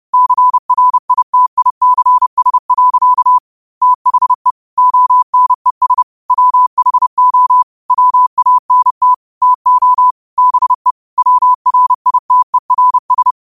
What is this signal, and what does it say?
Morse code, heard as a keyed tone, about 30 wpm, 1000 Hz. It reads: CRITICS. THE ONES WHO WANT TO BE WRITERS